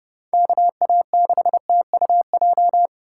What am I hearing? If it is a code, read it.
Morse code, heard as a keyed tone, about 30 wpm, 700 Hz. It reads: KA6TUJ
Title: USA Call Signs - 6 Characters - 4x Character Spacing - 30wpm